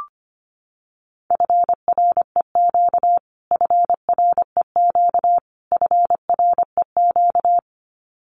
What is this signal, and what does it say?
Morse code, heard as a keyed tone, about 25 wpm, 700 Hz. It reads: FREQ FREQ FREQ
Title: Abbreviations Repeated 3x - 25wpm